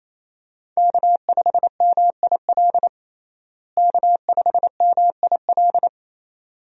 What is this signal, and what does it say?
Morse code, heard as a keyed tone, about 28 wpm, 700 Hz. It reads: K5MIL K5MIL
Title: USA Call Signs - 5 Characters - Repeated 2x - 28wpm